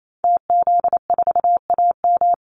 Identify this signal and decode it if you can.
Morse code, heard as a keyed tone, about 28 wpm, 700 Hz. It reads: TZ4AM
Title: Call Signs - 28wpm